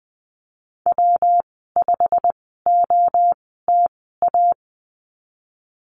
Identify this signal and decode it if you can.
Morse code, heard as a keyed tone, about 20 wpm, spaced as for 15 wpm, 700 Hz. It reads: W5OTA
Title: 128 Lesson 21g - Number 6 - Call Signs 20wpm